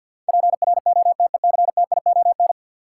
Morse code, heard as a keyed tone, about 47 wpm, 700 Hz. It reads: PROTECTION
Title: Sentences from Top 1600 Words - New Word Drill Alt - 50wpm